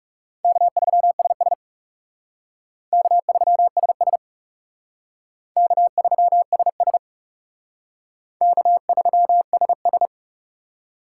Morse code, heard as a keyed tone, about 38 wpm, 700 Hz. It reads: K3SS K3SS K3SS K3SS